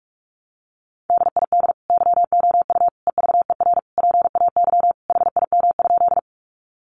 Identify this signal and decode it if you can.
Morse code, heard as a keyed tone, about 44 wpm, 700 Hz. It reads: DID YOU EVER PAY HIM?
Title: Sentences from Top 700 Words 45wpm